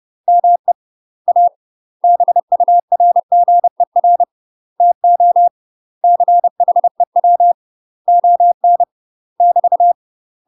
Morse code, heard as a keyed tone, about 30 wpm, 700 Hz. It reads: ME A BURGER TO CHEW ON <BT>
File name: UK MASTER QSOs TRAINING / 30WPM / UK MASTER QSOs 0051